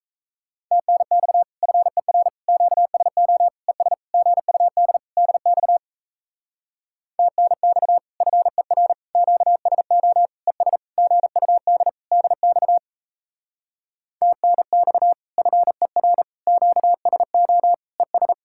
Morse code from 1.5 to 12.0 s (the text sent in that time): FER QSO ES GUD DX TNX FER QSO ES GUD